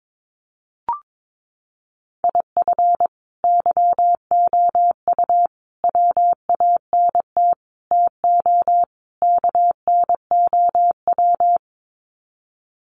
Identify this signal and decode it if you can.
Morse code, heard as a keyed tone, about 22 wpm, 700 Hz. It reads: IF YOU WANT TO KNOW
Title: Sets of 5 Words Encore2 22wpm